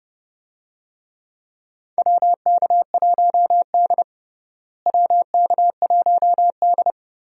Morse code, heard as a keyed tone, about 30 wpm, 700 Hz. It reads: WK1D WK1D
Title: USA Call Signs - 4 Characters - Repeated 2x - 30wpm